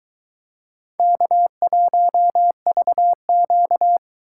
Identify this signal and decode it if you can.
Morse code, heard as a keyed tone, about 21 wpm, 700 Hz. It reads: K1VQ